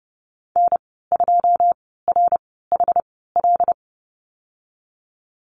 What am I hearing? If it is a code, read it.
Morse code, heard as a keyed tone, about 30 wpm, spaced as for 17 wpm, 700 Hz. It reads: N2RHL
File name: Lesson 15g - Number 5 - Call Signs 30wpm